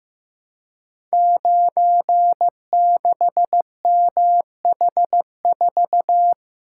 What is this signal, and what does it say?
Morse code, heard as a keyed tone, about 15 wpm, 700 Hz. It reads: 96MH4
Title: Code Groups - 5 Letter:Numbers - 15wpm